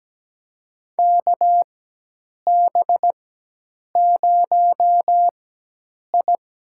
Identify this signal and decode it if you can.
Morse code, heard as a keyed tone, about 17 wpm, spaced as for 8 wpm, 700 Hz. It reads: KB0I
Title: USA Call Signs - 4x Character Spacing - 17wpm